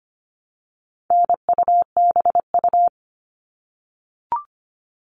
Morse code, heard as a keyed tone, about 25 wpm, 700 Hz. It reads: NUBU
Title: All Cut Numbers - 4-Digit 25wpm